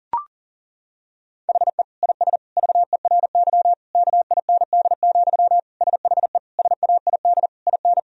Morse code, heard as a keyed tone, about 37 wpm, 700 Hz. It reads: HE IS VERY KIND, SHE SAID IN
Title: Sentences from Top 1000 Words - Review Part 1 - 40wpm